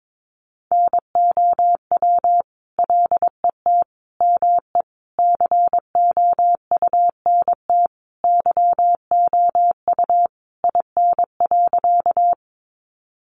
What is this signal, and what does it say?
Morse code, heard as a keyed tone, about 22 wpm, 700 Hz. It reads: NOW LET ME COUNT YOU IN.
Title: Sentences from Top 1500 Words 22wpm